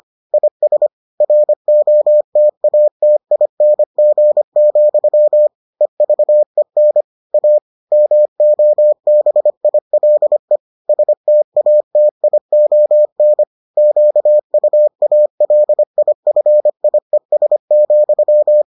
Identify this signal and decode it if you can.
Morse code, heard as a keyed tone, about 25 wpm, 600 Hz. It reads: IS ROTATING, EVEN A MOBILE STATION QUALIFIES,